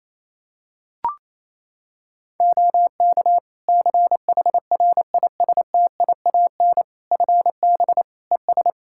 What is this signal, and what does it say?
Morse code, heard as a keyed tone, about 28 wpm, 700 Hz. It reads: OK CHRISTIAN FB ES
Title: Practice QSO - Exchange 2 - Info - 28wpm